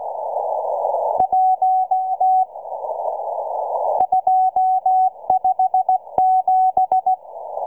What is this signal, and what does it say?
Morse code, heard as a keyed tone, about 16 wpm, 750 Hz. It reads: PT 257